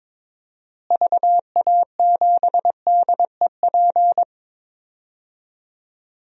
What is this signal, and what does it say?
Morse code, heard as a keyed tone, about 22 wpm, 700 Hz. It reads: VA7DEP
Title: DXCC-100 Call Signs - 6 Characters - Repeated 3x - 22wpm